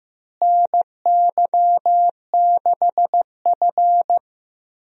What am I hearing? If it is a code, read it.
Morse code, heard as a keyed tone, about 15 wpm, 700 Hz. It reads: NY6F
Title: USA Call Signs - 4 Characters - Repeated 3x - 15wpm